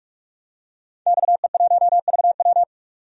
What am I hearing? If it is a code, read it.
Morse code, heard as a keyed tone, about 46 wpm, 700 Hz. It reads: XE1VW
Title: DXCC-100 Call Signs - 5 Characters - 45wpm